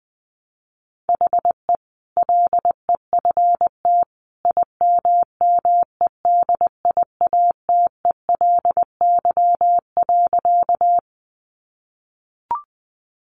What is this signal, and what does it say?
Morse code, heard as a keyed tone, about 20 wpm, 700 Hz. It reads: HE LEFT IMMEDIATELY.
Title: Sentences from Top 1100 Words 20wpm